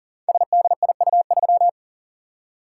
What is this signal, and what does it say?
Morse code, heard as a keyed tone, about 42 wpm, 700 Hz. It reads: SDIU3
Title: Code Groups - 5 Letter:Numbers - 40wpm